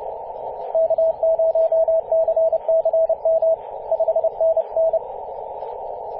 Morse code, heard as a keyed tone, about 30 wpm, 650 Hz, with radio noise. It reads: K0CCM 5NN
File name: 5w0xg-18-075mhz-cw